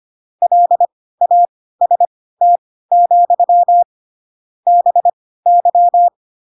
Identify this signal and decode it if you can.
Morse code, heard as a keyed tone, about 25 wpm, spaced as for 16 wpm, 700 Hz. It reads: LAST, BY
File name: Pinocchio / Pinocchio ch3 10 25 ik1hge